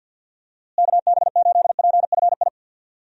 Morse code, heard as a keyed tone, about 45 wpm, 700 Hz. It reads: KB8PFI
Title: USA Call Signs - 6 Characters - Repeated 2x - 50wpm